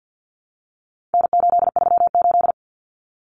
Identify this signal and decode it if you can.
Morse code, heard as a keyed tone, about 49 wpm, 700 Hz. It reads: N838
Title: Common Cut Numbers - 4-Digit 50wpm